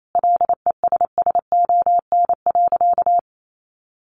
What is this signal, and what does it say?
Morse code, heard as a keyed tone, about 28 wpm, 700 Hz. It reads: LESSON.
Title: Sentences from Top 1200 Words 28wpm